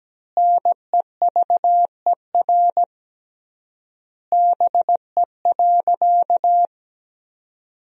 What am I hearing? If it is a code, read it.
Morse code, heard as a keyed tone, about 17 wpm, 700 Hz. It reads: NEVER BE.
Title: 3x Word-Spacing - Sentences from Top 300 Words - 17wpm